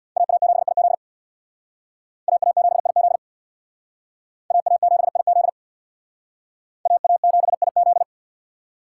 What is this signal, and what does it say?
Morse code, heard as a keyed tone, about 61 wpm, 700 Hz. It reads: AA7IZ AA7IZ AA7IZ AA7IZ
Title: Speed Racing - USA 2x2 Call Signs 50wpm